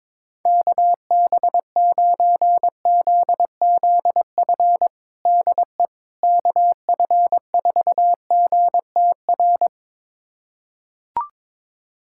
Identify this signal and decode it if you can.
Morse code, heard as a keyed tone, about 22 wpm, 700 Hz. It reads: KB9ZZF DE KF4GTR
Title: USA Callsign DE Callsign - 6 Characters - 22wpm